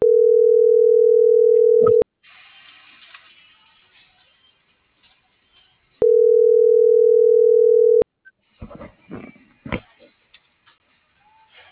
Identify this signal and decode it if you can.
Ringback tone (North America), 440 + 480 Hz.